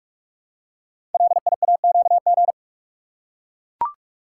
Morse code, heard as a keyed tone, about 44 wpm, 690 Hz. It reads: LIAQG
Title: Code Groups - 5 Letters - 45wpm